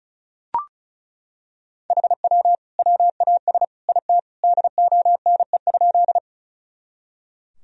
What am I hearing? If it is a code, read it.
Morse code, heard as a keyed tone, about 36 wpm, 700 Hz. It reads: HW WAS IT DONE?
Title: Sentences from Top 500 Words and Abbreviations 35wpm